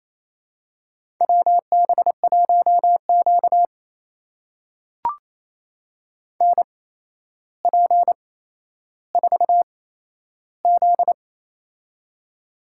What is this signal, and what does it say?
Morse code, heard as a keyed tone, about 28 wpm, spaced as for 12 wpm, 700 Hz. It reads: WB1Q NP4Z
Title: USA Call Signs - 4 Characters - 8x Character Spacing - 28wpm